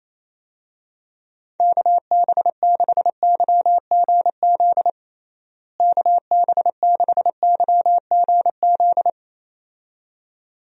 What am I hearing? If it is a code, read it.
Morse code, heard as a keyed tone, about 27 wpm, 700 Hz. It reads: KB6YGZ KB6YGZ
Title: USA Call Signs - 6 Characters - Repeated 2x - 28wpm